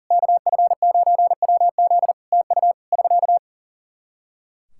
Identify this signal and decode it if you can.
Morse code, heard as a keyed tone, about 40 wpm, 700 Hz. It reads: KF9WZ TU <SK>
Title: Practice QSO - Exchange 4 - Ending - 40wpm